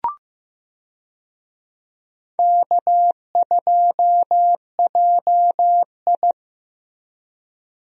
Morse code, heard as a keyed tone, about 15 wpm, 700 Hz. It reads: K2JI
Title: USA Call Signs - 4 Characters - Repeated 3x - 15wpm